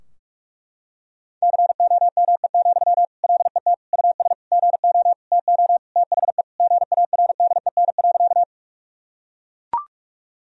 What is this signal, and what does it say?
Morse code, heard as a keyed tone, about 41 wpm, 700 Hz. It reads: COME, LET US GO TO THE GARDEN.
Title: Sentences from Top 1000 Words - Review Part 4 - 45wpm